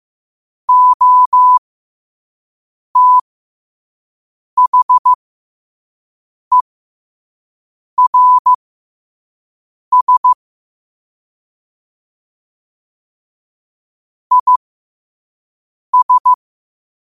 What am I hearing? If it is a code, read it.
Morse code, heard as a keyed tone, about 15 wpm, spaced as for 5 wpm, 1000 Hz. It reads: OTHERS IS
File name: qotdfiles / 5wpm